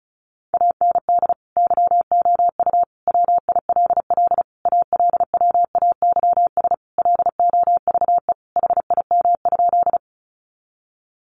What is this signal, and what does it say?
Morse code, heard as a keyed tone, about 34 wpm, 700 Hz. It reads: AND YOU WILL ALWAYS LOVE HIM?
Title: Sentences from Top 400 Words 35wpm